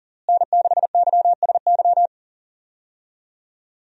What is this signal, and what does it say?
Morse code, heard as a keyed tone, about 40 wpm, 700 Hz. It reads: N6YSY
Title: USA Call Signs - 5 Characters - Repeated 2x - 40wpm